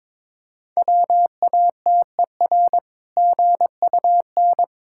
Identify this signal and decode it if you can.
Morse code, heard as a keyed tone, about 22 wpm, 690 Hz. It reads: WATER GUN
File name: Difficult Words with Hints - 22wpm